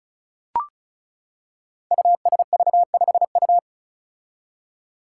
Morse code, heard as a keyed tone, about 36 wpm, 700 Hz. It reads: USV5U
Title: Five Challenging Letter-Number Combinations 35wpm